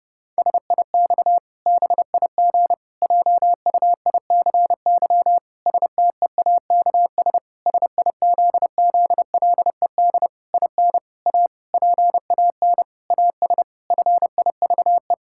Morse code, heard as a keyed tone, about 32 wpm, 700 Hz. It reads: SIX BIG JUICY STEAKS SIZZLED IN A PAN AS FIVE
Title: Pangram Challenge v2 - 30wpm